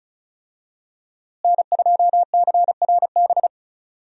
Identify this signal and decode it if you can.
Morse code, heard as a keyed tone, about 36 wpm, 700 Hz. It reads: N2CRB
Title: USA Call Signs - 5 Characters - Repeated 2x - 35wpm